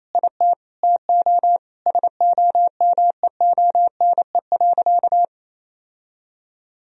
Morse code, heard as a keyed tone, about 28 wpm, 700 Hz. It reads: IT TO SOMEONE.
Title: Sentences from Top 1000 Words - Review Part 4 - 28wpm